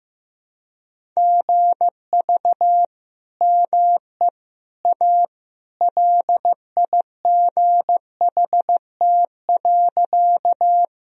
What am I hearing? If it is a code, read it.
Morse code, heard as a keyed tone, about 15 wpm, 700 Hz. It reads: GV ME A LIGHT.
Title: Sentences from Top 200 Words and Abbreviations 15wpm